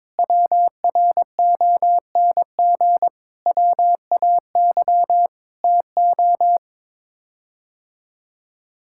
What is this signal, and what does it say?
Morse code, heard as a keyed tone, about 22 wpm, 700 Hz. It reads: WRONG WAY TO
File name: practice / 22wpm / Sets of 8 Words - 22wpm